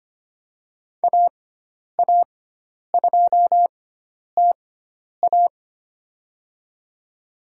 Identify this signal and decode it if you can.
Morse code, heard as a keyed tone, about 25 wpm, spaced as for 10 wpm, 700 Hz. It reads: AA2TA
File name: Lesson 16g - Letter c - Call Signs 25wpm